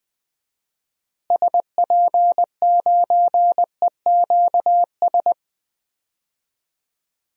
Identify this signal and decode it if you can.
Morse code, heard as a keyed tone, about 21 wpm, 700 Hz. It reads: SP9EQS
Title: DXCC-100 Call Signs - 6 Characters - Repeated 3x - 20wpm